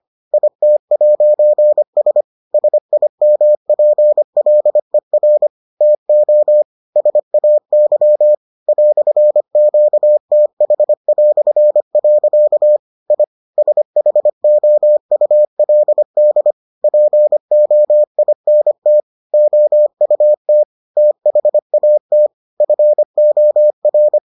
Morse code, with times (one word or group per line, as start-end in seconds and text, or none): 0.3-2.2 IT'S
2.5-5.5 SIMPLER
5.8-6.6 TO
7.0-8.3 SAY
8.7-12.8 "QTH".
13.1-13.2 I
13.6-16.5 SHOULD
16.8-19.0 POINT
19.3-20.6 OUT
21.0-22.3 THAT
22.6-24.2 FOR